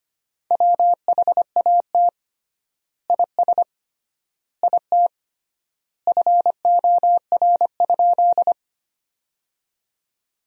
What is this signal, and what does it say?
Morse code, heard as a keyed tone, about 25 wpm, 700 Hz. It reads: WHAT IS IT FOR?
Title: 3x Word-Spacing - Sentences from Top 100 Words - 25wpm